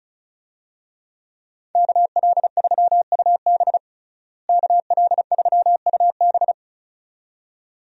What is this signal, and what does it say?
Morse code, heard as a keyed tone, about 34 wpm, 700 Hz. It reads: KL3UB KL3UB
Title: USA Call Signs - 5 Characters - Repeated 2x - 35wpm